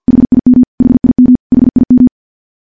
Morse code, heard as a keyed tone, about 50 wpm, 250 Hz. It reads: HIM HIM HIM
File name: morse / morse-training-50WPM